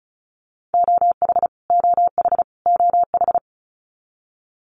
Morse code, heard as a keyed tone, about 34 wpm, 700 Hz. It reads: OH OH OH
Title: Top 500 Words Repeated 35wpm